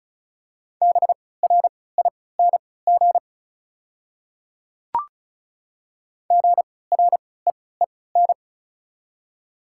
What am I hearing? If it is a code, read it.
Morse code, heard as a keyed tone, about 34 wpm, spaced as for 20 wpm, 690 Hz. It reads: BRING GREEN
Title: 3x Character Spacing - Top 500 Words - 35wpm